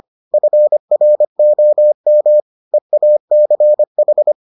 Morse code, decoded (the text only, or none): FROM EACH